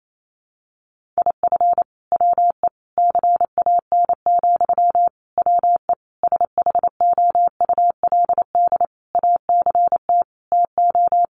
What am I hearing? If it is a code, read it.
Morse code, heard as a keyed tone, about 28 wpm, 700 Hz. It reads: IF WE CAN, WE SHOULD ACT TO
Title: Sentences from Top 1100 Words 28wpm